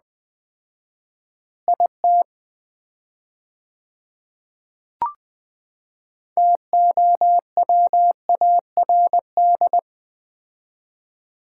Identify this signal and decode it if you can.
Morse code, heard as a keyed tone, about 21 wpm, 700 Hz. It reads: IT TOWARD
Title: 4x Word-Spacing - Sets of 5 Words - 20wpm